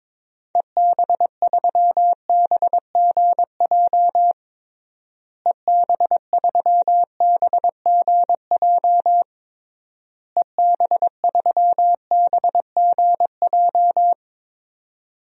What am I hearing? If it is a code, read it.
Morse code, heard as a keyed tone, about 22 wpm, 700 Hz. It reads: EB3BGJ EB3BGJ EB3BGJ
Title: DXCC-100 Call Signs - 6 Characters - Repeated 3x - 22wpm